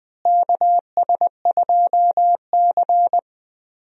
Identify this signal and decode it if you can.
Morse code, heard as a keyed tone, about 20 wpm, 700 Hz. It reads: KS2C
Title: USA Call Signs - 4 Characters - Repeated 2x - 20wpm